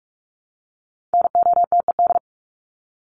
Morse code, heard as a keyed tone, about 44 wpm, 700 Hz. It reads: NOTED
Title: ED Suffix - Top 100 Words with Base Words 45wpm